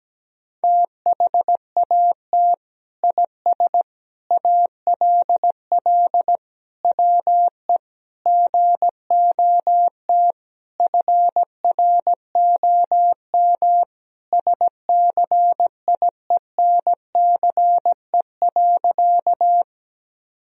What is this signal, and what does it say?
Morse code, heard as a keyed tone, about 17 wpm, 700 Hz. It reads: THAT IS ALL WE GOT FROM SCIENCE.
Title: Sentences from Top 1000 Words - Review Part 2 - 17wpm